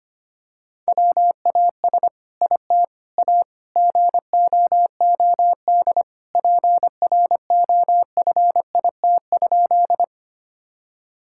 Morse code, heard as a keyed tone, about 25 wpm, 700 Hz. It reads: WAS IT A GOOD PROFIT?
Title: Sentences from Top 1000 Words - Review Part 5 - 25wpm